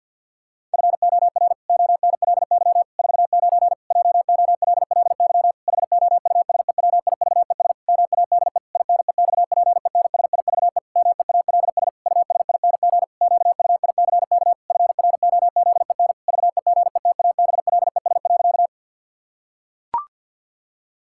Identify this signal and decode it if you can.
Morse code, heard as a keyed tone, about 48 wpm, 700 Hz. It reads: FOR ONLY 49 JOLLY HOUSEWIVES MADE INEXPENSIVE MEALS USING QUICK FROZEN VEGETABLES.
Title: Pangram Challenge v2 - 50wpm